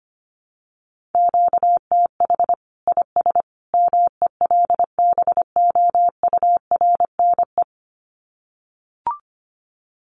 Morse code, heard as a keyed tone, about 25 wpm, 700 Hz. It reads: QTH IS MELBOURNE
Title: Common QSO Phrases 25wpm